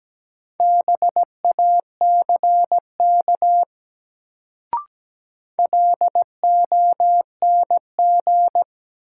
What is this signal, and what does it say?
Morse code, heard as a keyed tone, about 17 wpm, 700 Hz. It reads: BACK LONG
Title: Sets of 2 Words - New Word Drill - 17wpm